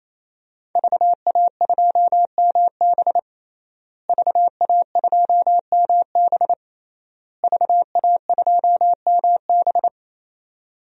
Morse code, heard as a keyed tone, about 28 wpm, 700 Hz. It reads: VA2MB VA2MB VA2MB
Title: DXCC-100 Call Signs - 5 Characters - Repeated 3x - 28wpm